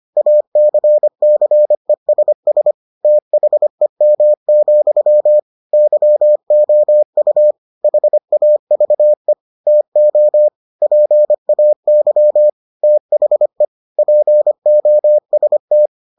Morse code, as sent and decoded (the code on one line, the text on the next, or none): .- -.-. -.-. ... ... / - .... -- --..-- / -.-- --- ..- / .... .- ...- . / - --- / .--. .- -.-- / - .... / .--. --- ... -
ACCESS THEM, YOU HAVE TO PAY THE POST